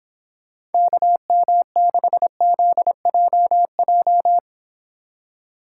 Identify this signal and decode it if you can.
Morse code, heard as a keyed tone, about 30 wpm, 700 Hz. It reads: KM6ZJJ